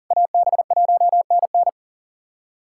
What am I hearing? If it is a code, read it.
Morse code, heard as a keyed tone, about 40 wpm, 700 Hz. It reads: AB1NN
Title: USA Callsign DE Callsign - 5 Characters - 40wpm